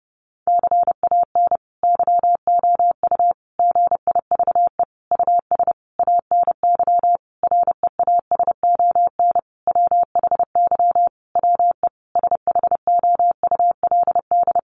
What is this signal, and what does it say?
Morse code, heard as a keyed tone, about 30 wpm, 700 Hz. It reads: CAN YOU GIVE US ANY REASON WHY WE SHOULD
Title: Sentences from Top 1100 Words 30wpm